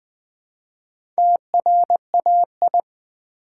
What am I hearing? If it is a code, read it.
Morse code, heard as a keyed tone, about 20 wpm, 700 Hz. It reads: TRAI
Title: Build A Word - 1 Letter At A Time - 5-Letter Words - 20wpm